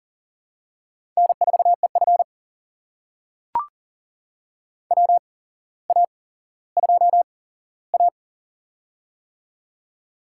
Morse code, heard as a keyed tone, about 40 wpm, spaced as for 17 wpm, 700 Hz. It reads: N4EF WA2A
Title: USA Call Signs - 4 Characters - 8x Character Spacing - 40wpm